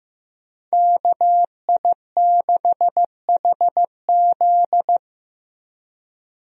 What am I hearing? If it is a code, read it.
Morse code, heard as a keyed tone, about 15 wpm, 700 Hz. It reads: KI6HZ
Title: USA Call Signs - 5 Characters - Repeated 3x - 15wpm